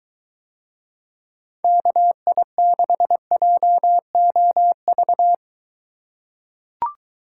Morse code, heard as a keyed tone, about 22 wpm, 700 Hz. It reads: KI6JOV